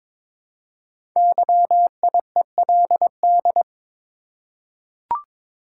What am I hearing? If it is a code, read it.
Morse code, heard as a keyed tone, about 23 wpm, 700 Hz. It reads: YIELD
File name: 5 Letter Words - First 3 Letters Given - 22wpm